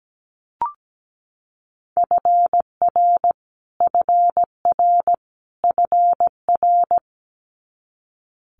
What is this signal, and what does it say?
Morse code, heard as a keyed tone, about 17 wpm, 700 Hz. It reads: FR FR FR
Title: Multi-letter Phonemes Repeated 3x 17wpm